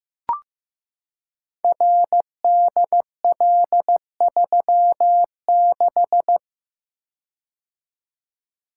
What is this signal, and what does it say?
Morse code, heard as a keyed tone, about 15 wpm, 700 Hz. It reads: RDL36
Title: Code Groups - 5 Letter:Numbers - 15wpm